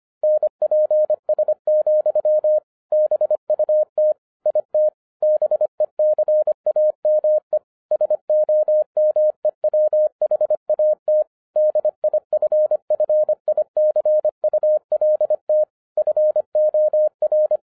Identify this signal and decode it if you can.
Morse code, heard as a keyed tone, about 25 wpm, 610 Hz. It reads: NPS, BUT IT BECAME SOMEWHAT DIFFICULT FOR
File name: 2011-07-mp3 / n39 THE 3.11 2 25wpm